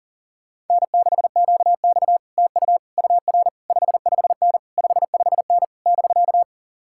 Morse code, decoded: N6QX TU UR 55N 55N <BK>

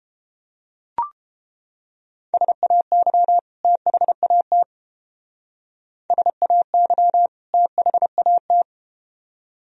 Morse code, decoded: SAY THAT SAY THAT

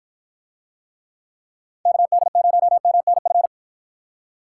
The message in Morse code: KD0MNF

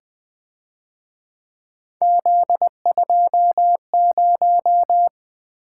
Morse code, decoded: Z20

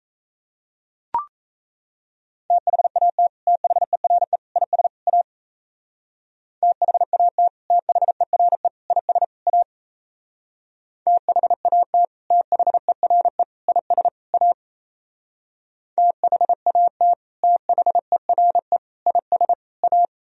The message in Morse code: THAT THERE IS A THAT THERE IS A THAT THERE IS A THAT THERE IS A

